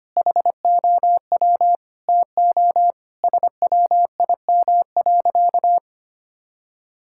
Morse code, heard as a keyed tone, about 25 wpm, 700 Hz. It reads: HOW TO SWIM.
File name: practice / 25wpm / Sentences with 4-Word Phrases - 25wpm